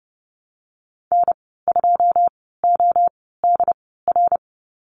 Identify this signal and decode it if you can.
Morse code, heard as a keyed tone, about 30 wpm, spaced as for 17 wpm, 700 Hz. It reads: N2ODR
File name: Lesson 14g - Number 2 - Call Signs 30wpm